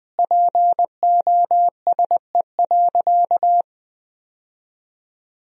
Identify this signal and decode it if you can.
Morse code, heard as a keyed tone, about 21 wpm, 700 Hz. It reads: POSE.
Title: Sentences from Top 1000 Words - Review Part 2 - 20wpm